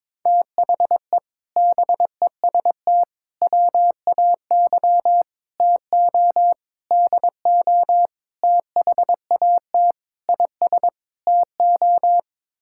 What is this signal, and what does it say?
Morse code, heard as a keyed tone, about 22 wpm, 700 Hz. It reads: THE BEST WAY TO DO THAT IS TO
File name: practice / 22wpm / Sets of 8 Words - 22wpm